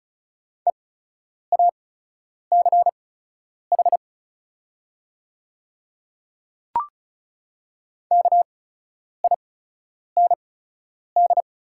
Morse code, heard as a keyed tone, about 34 wpm, spaced as for 10 wpm, 700 Hz. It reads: EACH KIND
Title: 8x Character Spacing - Top 300 Words - 35wpm